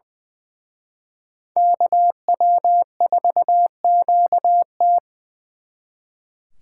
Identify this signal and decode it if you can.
Morse code, heard as a keyed tone, about 24 wpm, 700 Hz. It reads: KW4QT
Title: Speed Racing - USA Call Signs 20wpm